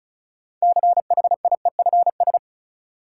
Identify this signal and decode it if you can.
Morse code, heard as a keyed tone, about 36 wpm, 700 Hz. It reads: CHIEFS
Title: Sets of 5 Words Encore - New Word Drill - 35wpm